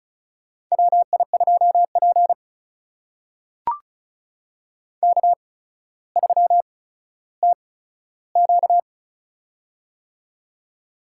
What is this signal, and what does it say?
Morse code, heard as a keyed tone, about 36 wpm, spaced as for 15 wpm, 700 Hz. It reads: WI2P K3TQ
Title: USA Call Signs - 4 Characters - 8x Character Spacing - 35wpm